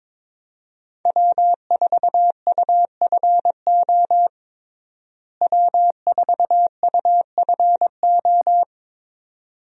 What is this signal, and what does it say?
Morse code, heard as a keyed tone, about 23 wpm, 700 Hz. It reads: W4UFO W4UFO
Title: USA Call Signs - 5 Characters - Repeated 2x - 22wpm